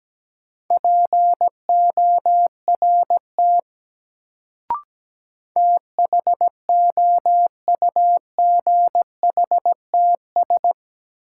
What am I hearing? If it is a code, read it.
Morse code, heard as a keyed tone, about 17 wpm, 700 Hz. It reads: PORT THOUGHTS
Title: Sentences from Top 1000 Words - Review Part 1 - New Word Drill - 17wpm